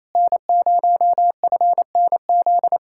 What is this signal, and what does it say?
Morse code, heard as a keyed tone, about 26 wpm, 700 Hz. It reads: N0FNZ
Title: Speed-Racing Common QSO Phrases Encore - 28wpm